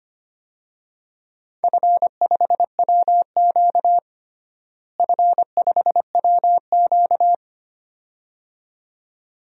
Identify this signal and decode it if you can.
Morse code, heard as a keyed tone, about 25 wpm, 700 Hz. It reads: F5WQ F5WQ
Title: DXCC-100 Call Signs - 4 Characters - Repeated 2x - 25wpm